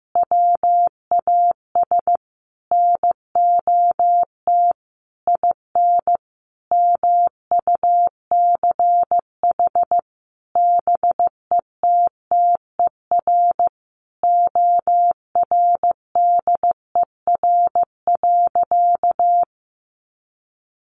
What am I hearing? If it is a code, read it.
Morse code, heard as a keyed tone, about 15 wpm, 700 Hz. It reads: WAS NOT IN MUCH BETTER ORDER.
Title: Sentences from Top 1500 Words 15wpm